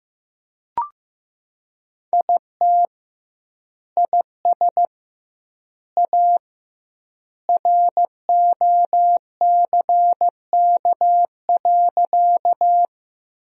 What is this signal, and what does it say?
Morse code, heard as a keyed tone, about 15 wpm, 700 Hz. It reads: IT IS A ROCK.
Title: 2x Word-Spacing - Sentences from Top 400 Words - 15wpm